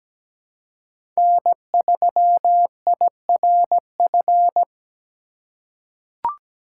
Morse code, heard as a keyed tone, about 17 wpm, 700 Hz. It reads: N3IRF
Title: USA Call Signs - 5 Characters - 4x Character Spacing - 17wpm